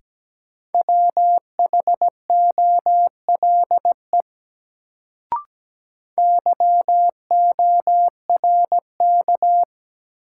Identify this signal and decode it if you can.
Morse code, heard as a keyed tone, about 17 wpm, 700 Hz. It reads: WHOLE YORK
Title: Sets of 2 Words - New Word Drill - 17wpm